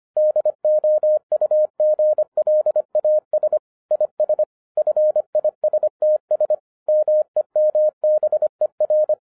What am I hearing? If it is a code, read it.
Morse code, heard as a keyed tone, about 24 wpm, 610 Hz. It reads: DOUGLAS IS FISTS MEMBER
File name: 2012-01-mp3 / n06-FIRST FISTS CONTACT-25wpm